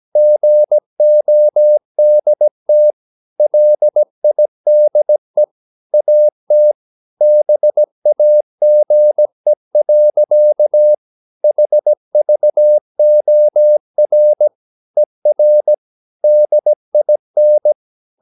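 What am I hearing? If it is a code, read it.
Morse code, heard as a keyed tone, about 17 wpm, 600 Hz. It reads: GODT LIDE AT BAGE. HVOR ER DIN
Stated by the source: Slow (QRS) Morse practice at 17 wpm, from CW med Gnister.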